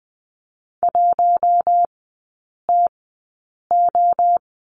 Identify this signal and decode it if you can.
Morse code, heard as a keyed tone, about 20 wpm, spaced as for 8 wpm, 700 Hz. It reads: 1TO
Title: 041 Lesson 8e - Number 1 - Three Characters 20wpm